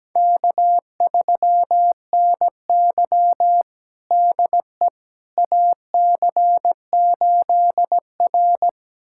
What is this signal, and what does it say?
Morse code, heard as a keyed tone, about 17 wpm, 700 Hz. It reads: K3NY DE AC8R
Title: USA Callsign DE Callsign - 4 Characters - 17wpm